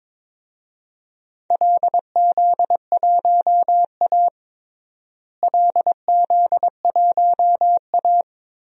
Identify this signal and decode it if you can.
Morse code, heard as a keyed tone, about 22 wpm, 700 Hz. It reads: LZ1A LZ1A
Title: DXCC-100 Call Signs - 4 Characters - Repeated 3x - 22wpm